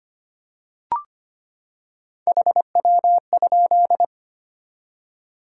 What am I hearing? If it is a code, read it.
Morse code, heard as a keyed tone, about 25 wpm, 700 Hz. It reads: HW?
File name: practice / 25wpm / Top 20 QSO Elements - 25wpm